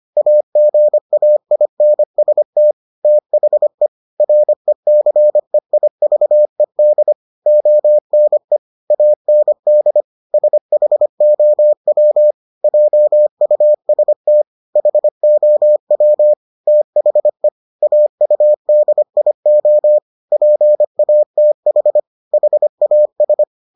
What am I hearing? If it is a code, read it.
Morse code, heard as a keyed tone, about 25 wpm, 600 Hz. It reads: AGAINST THE RECEIVED ONE AND SHOW JUST HOW THE AUDIO PATH HAS